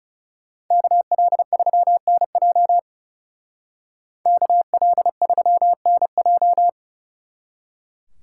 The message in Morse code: KL3NJ KL3NJ

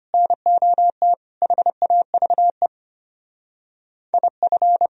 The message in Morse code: NOT HAVE IF

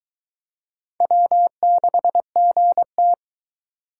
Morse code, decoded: W6GT

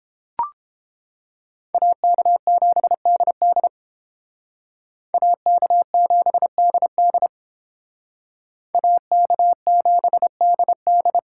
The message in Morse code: AK7DD AK7DD AK7DD

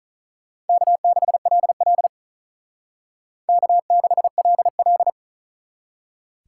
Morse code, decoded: K6LL K6LL

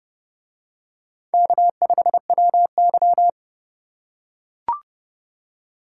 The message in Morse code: K5WY

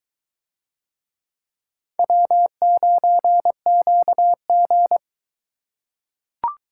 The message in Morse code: W9QG

